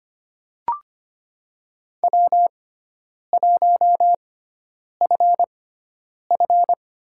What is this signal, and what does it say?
Morse code, heard as a keyed tone, about 25 wpm, spaced as for 9 wpm, 700 Hz. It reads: W1FF
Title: USA Call Signs - 6x Character Spacing - 25wpm